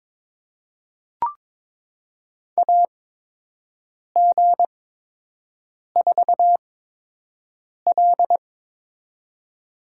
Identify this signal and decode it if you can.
Morse code, heard as a keyed tone, about 22 wpm, spaced as for 6 wpm, 700 Hz. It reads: AG4L
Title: USA Call Signs - 4 Characters - 8x Character Spacing - 22wpm